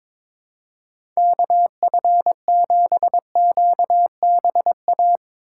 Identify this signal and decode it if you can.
Morse code, heard as a keyed tone, about 22 wpm, 700 Hz. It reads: KF7QBA
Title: USA Call Signs - 6 Characters - 6x Character Spacing - 22wpm